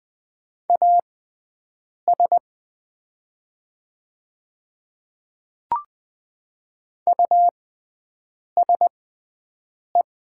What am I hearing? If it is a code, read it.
Morse code, heard as a keyed tone, about 20 wpm, spaced as for 7 wpm, 700 Hz. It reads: AS USE
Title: 6x Character Spacing - Top 200 Words - 20wpm